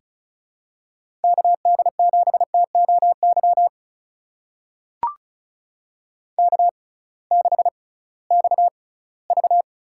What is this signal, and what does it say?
Morse code, heard as a keyed tone, about 34 wpm, spaced as for 20 wpm, 700 Hz. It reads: KD7TOY K6XV
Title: USA Call Signs - 6x Character Spacing - 35wpm